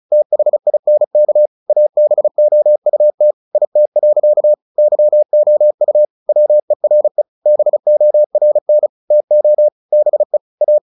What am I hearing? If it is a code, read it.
Morse code, heard as a keyed tone, about 34 wpm, 600 Hz. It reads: THINK ABOUT IT. YOU WERE BORN TO BE A
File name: the old man and the sea 2 35wpm